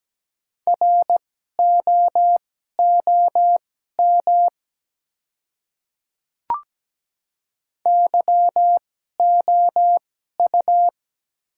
Morse code, heard as a keyed tone, about 17 wpm, spaced as for 12 wpm, 700 Hz. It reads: ROOM YOU
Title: 2x Character Spacing - Top 300 Words - 17wpm